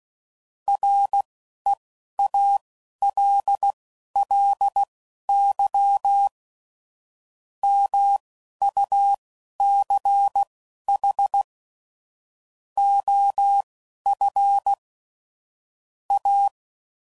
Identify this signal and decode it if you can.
Morse code, heard as a keyed tone, about 16 wpm, spaced as for 11 wpm, 800 Hz. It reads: REALLY MUCH OF A
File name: CW Code Groups @15WPM / QSOs / WD0BGP Spaced